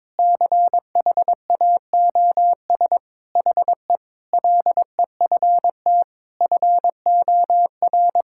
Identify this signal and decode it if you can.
Morse code, heard as a keyed tone, about 21 wpm, 700 Hz. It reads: CHAOS HE LEFT FOR
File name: practice / 22wpm / Sentences with 4-Word Phrases - 22wpm